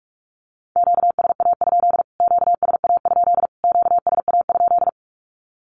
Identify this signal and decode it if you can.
Morse code, heard as a keyed tone, about 44 wpm, 700 Hz. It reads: QSA? QSA? QSA?
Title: Q-Codes Repeated 3x 45wpm